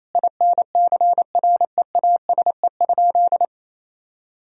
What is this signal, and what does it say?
Morse code, heard as a keyed tone, about 28 wpm, 700 Hz. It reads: INCREASE?
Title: Sentences from Top 1000 Words - Review Part 4 - 28wpm